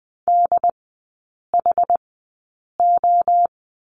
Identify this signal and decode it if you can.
Morse code, heard as a keyed tone, about 20 wpm, spaced as for 8 wpm, 700 Hz. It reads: DHO
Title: 212 Lesson 34e - Number 8 - Three Characters 20wpm